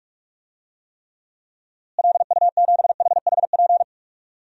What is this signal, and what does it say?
Morse code, heard as a keyed tone, about 47 wpm, 700 Hz. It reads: LU7HHP
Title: DXCC-100 Call Signs - 6 Characters - Repeated 2x - 45wpm